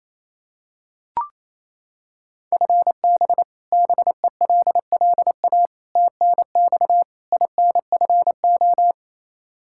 Morse code, heard as a keyed tone, about 28 wpm, 700 Hz. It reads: FB BELLA TNX INFO
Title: Practice QSO - Exchange 3 - More Info - 28wpm